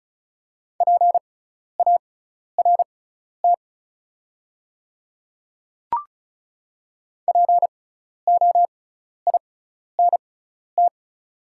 Morse code, heard as a keyed tone, about 36 wpm, spaced as for 12 wpm, 700 Hz. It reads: PART POINT
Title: 6x Character Spacing - Top 200 Words - 35wpm